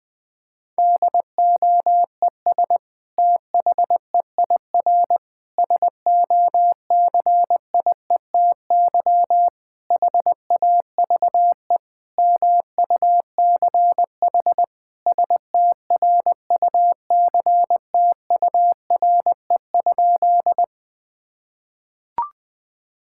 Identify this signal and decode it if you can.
Morse code, heard as a keyed tone, about 20 wpm, 700 Hz. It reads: DOES THEIR SOCIETY HAVE MUCH STRUCTURE?
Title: Sentences from Top 1000 Words - Review Part 6 - 20wpm